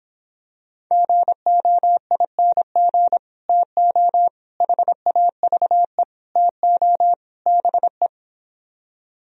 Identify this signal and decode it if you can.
Morse code, heard as a keyed tone, about 26 wpm, 700 Hz. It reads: GOING TO HAVE TO BE